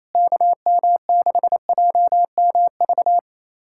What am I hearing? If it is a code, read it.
Morse code, heard as a keyed tone, about 28 wpm, 700 Hz. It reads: KM6JMV
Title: USA Callsign DE Callsign - 6 Characters - 28wpm